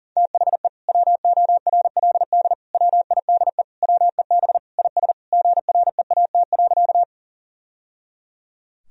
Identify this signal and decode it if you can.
Morse code, heard as a keyed tone, about 40 wpm, 700 Hz. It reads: THE WORLD WIDE WEB IS GREAT.
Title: Sentences from Top 1000 Words - Review Part 4 - 40wpm